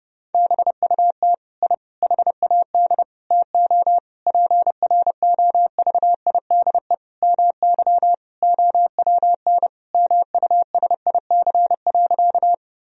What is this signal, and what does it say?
Morse code, heard as a keyed tone, about 30 wpm, 700 Hz. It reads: BUT I HAD TO PROVIDE MY OWN MUSIC.
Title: Sentences from Top 1000 Words - Review Part 3 - 30wpm